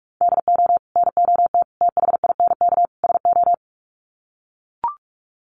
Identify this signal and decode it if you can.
Morse code, heard as a keyed tone, about 44 wpm, 700 Hz. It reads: DO NOT THINK SO
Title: Sets of 4 Words 45wpm